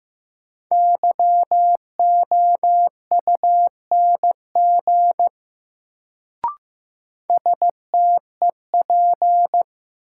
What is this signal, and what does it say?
Morse code, heard as a keyed tone, about 15 wpm, 700 Hz. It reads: YOUNG STEP
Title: Sentences from Top 400 Words - New Word Drill - 15wpm